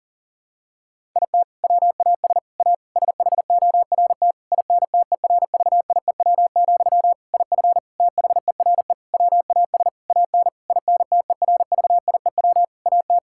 Morse code, heard as a keyed tone, about 37 wpm, 700 Hz. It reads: IT WAS A SHORT INTERVIEW, IF THERE WAS AN INTERVIEW AT
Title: Sentences from Top 1600 Words 40wpm